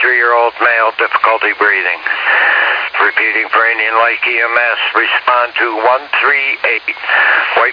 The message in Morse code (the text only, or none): none